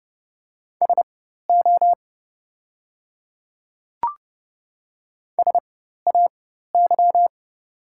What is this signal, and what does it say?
Morse code, heard as a keyed tone, about 30 wpm, spaced as for 14 wpm, 700 Hz. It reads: SO SAY
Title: 4x Character Spacing - Top 300 Words - 30wpm